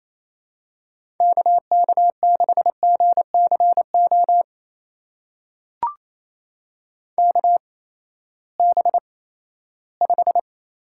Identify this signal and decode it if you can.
Morse code, heard as a keyed tone, about 28 wpm, spaced as for 16 wpm, 700 Hz. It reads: KK6GCO KB5
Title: USA Call Signs - 6 Characters - 8x Character Spacing - 28wpm